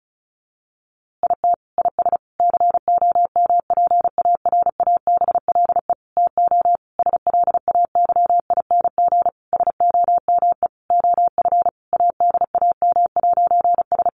Morse code, heard as a keyed tone, about 34 wpm, 700 Hz. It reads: IT IS COMPARABLE TO SLAYING SOME OF ADAM'S
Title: A Sound Of Thunder 35wpm